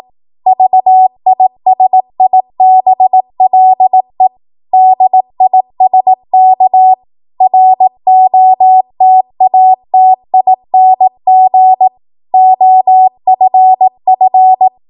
Morse code, heard as a keyed tone, about 18 wpm, 750 Hz. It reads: VISIBLE DISK ROTATING OFF
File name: Morse / 18WPM